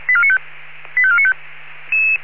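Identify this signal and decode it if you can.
Five-tone selective call (ZVEI), heard as quick runs of five tones, tones of about 70 ms. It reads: 85495
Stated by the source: ZVEI five-tone selective call received on an SDR.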